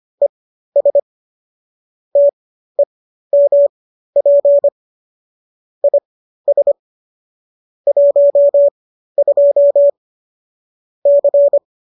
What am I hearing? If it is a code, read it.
Morse code, heard as a keyed tone, about 25 wpm, spaced as for 13 wpm, 570 Hz. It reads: ES TEMP IS 12 C
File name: qso214 13